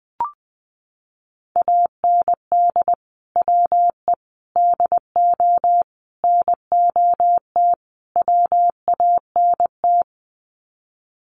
Morse code, heard as a keyed tone, about 20 wpm, 700 Hz. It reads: AND WE DO NOT WANT
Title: Sets of 5 Words Encore2 20wpm